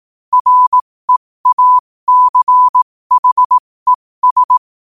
Morse code, heard as a keyed tone, about 18 wpm, spaced as for 15 wpm, 1000 Hz. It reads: REACHES
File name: qotdfiles / 15wpm